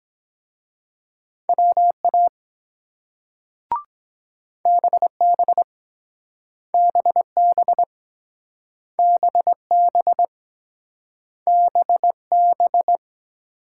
Morse code, heard as a keyed tone, about 22 wpm, 700 Hz. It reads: WA BB BB BB BB